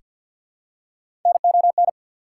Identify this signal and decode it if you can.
Morse code, heard as a keyed tone, about 48 wpm, 690 Hz. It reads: NON
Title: Prefixes and Suffixes 50wpm